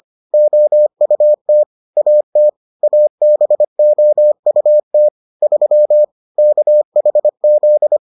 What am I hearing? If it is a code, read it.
Morse code, heard as a keyed tone, about 25 wpm, 600 Hz. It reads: OUT AT ABOUT 3 KHZ